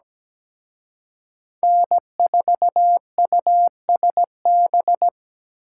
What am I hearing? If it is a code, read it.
Morse code, heard as a keyed tone, about 20 wpm, 700 Hz. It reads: N4USB
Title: Speed Racing - USA 1x3 Call Signs 17wpm